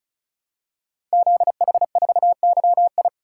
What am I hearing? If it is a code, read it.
Morse code, heard as a keyed tone, about 36 wpm, 700 Hz. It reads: ZH4YI
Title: Code Groups - 5 Letter:Numbers - 35wpm